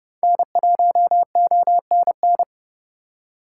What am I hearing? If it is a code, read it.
Morse code, heard as a keyed tone, about 31 wpm, 700 Hz. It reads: N1ONN
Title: USA Call Signs - 5 Characters - 3x Character Spacing - 30wpm